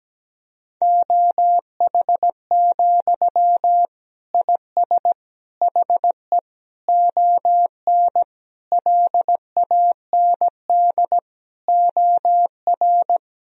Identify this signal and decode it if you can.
Morse code, heard as a keyed tone, about 17 wpm, 700 Hz. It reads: OH, IS HE ON LAND OR
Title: Sentences from Top 1000 Words - Review Part 1 - 17wpm